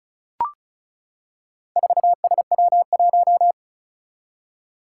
Morse code, heard as a keyed tone, about 36 wpm, 700 Hz. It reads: 4SW1
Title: Four Challenging Letter-Number Combinations 35wpm